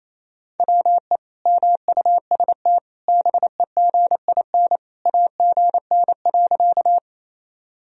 Morse code, heard as a keyed tone, about 28 wpm, 700 Hz. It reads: WE MUST BEGIN AGN.
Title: Sentences from Top 300 Words and Abbreviations 28wpm